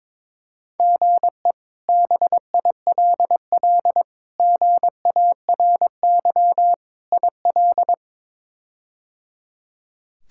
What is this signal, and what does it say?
Morse code, heard as a keyed tone, about 21 wpm, 700 Hz. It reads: GE BILL GARY IL
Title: K1USN SST QSO Practice - 22wpm